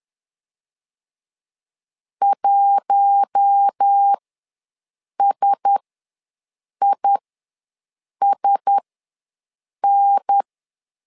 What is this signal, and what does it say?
Morse code, heard as a keyed tone, about 10 wpm, spaced as for 6 wpm, 800 Hz. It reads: 1SISN
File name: LEZIONI / 4 Q1S ws 10wpm 2208